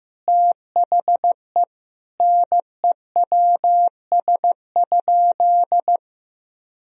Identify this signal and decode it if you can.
Morse code, heard as a keyed tone, about 15 wpm, 700 Hz. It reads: THE NEWS?
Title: Sentences from Top 1000 Words - Review Part 4 - 15wpm